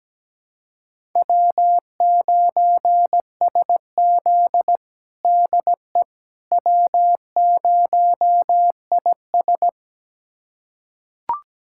Morse code, heard as a keyed tone, about 17 wpm, 700 Hz. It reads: W9SZ DE W0IS
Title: USA Callsign DE Callsign - 4 Characters - 17wpm